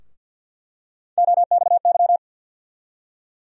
Morse code, heard as a keyed tone, about 48 wpm, 700 Hz. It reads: OXY
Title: USA Call Sign Postfixes - 3 Letters 50wpm